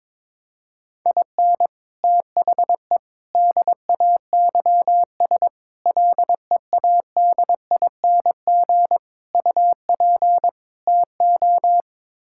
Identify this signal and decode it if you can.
Morse code, heard as a keyed tone, about 22 wpm, 700 Hz. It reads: IN THE DAYS LEADING UP TO
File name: practice / 22wpm / Sets of 6 Words - 22wpm